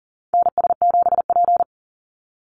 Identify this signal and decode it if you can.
Morse code, heard as a keyed tone, about 40 wpm, 700 Hz. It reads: NS7P
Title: NA SOTA Activator Callsigns 40wpm